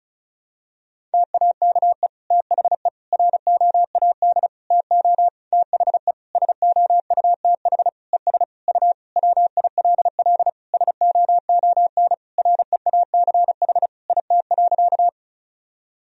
Morse code, heard as a keyed tone, about 36 wpm, 700 Hz. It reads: TAKE THE ROAD TO THE SOUTH ES U WILL SOON REACH IT.
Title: Sentences from Top 400 Words and Abbreviations 35wpm